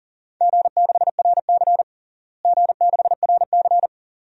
Morse code, heard as a keyed tone, about 40 wpm, 690 Hz. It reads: G6RC G6RC
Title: DXCC-100 Call Signs - 4 Characters - Repeated 2x - 40wpm